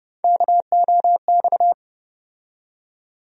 Morse code, heard as a keyed tone, about 31 wpm, 700 Hz. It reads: KOX
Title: USA Call Sign Postfixes - 3 Letters 30wpm